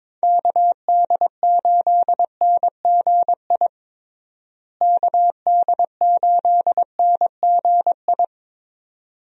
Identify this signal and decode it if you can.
Morse code, heard as a keyed tone, about 22 wpm, 700 Hz. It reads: KD8NGI KD8NGI
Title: USA Call Signs - 6 Characters - Repeated 2x - 22wpm